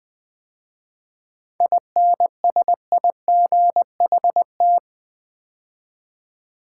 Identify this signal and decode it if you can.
Morse code, heard as a keyed tone, about 20 wpm, 700 Hz. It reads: INSIGHT
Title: 7 Letter Words - First 3 Letters Given - 20wpm